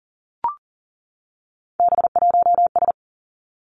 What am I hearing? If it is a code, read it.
Morse code, heard as a keyed tone, about 43 wpm, 700 Hz. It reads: B1S
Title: Three Challenging Letter-Number Combinations 40wpm